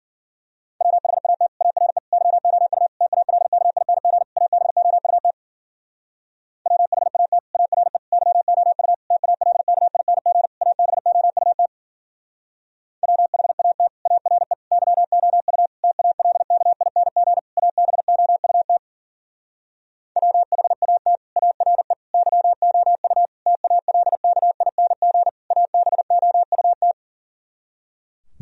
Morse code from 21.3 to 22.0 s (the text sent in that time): ARE